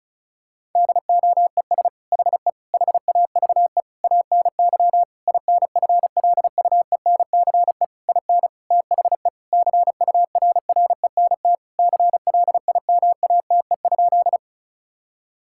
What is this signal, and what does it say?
Morse code, heard as a keyed tone, about 36 wpm, 700 Hz. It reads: DOES HE HAVE ANY INFLUENCE IN THE CURRENT CLIMATE?
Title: Sentences from Top 1600 Words 35wpm